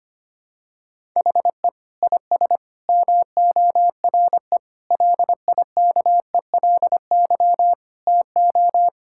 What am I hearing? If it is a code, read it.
Morse code, heard as a keyed tone, about 25 wpm, 700 Hz. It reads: HE IS MORE LIKELY TO
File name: practice / 25wpm / Sentences with 3-Word Phrases - 25wpm